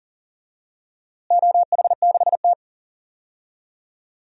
Morse code, heard as a keyed tone, about 37 wpm, 700 Hz. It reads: OH6T
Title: DXCC-100 Call Signs - 4 Characters - Repeated 3x - 40wpm